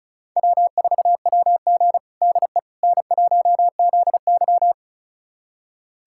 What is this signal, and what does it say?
Morse code, heard as a keyed tone, about 36 wpm, 700 Hz. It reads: W4WG DE N1ZY
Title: USA Callsign DE Callsign - 4 Characters - 35wpm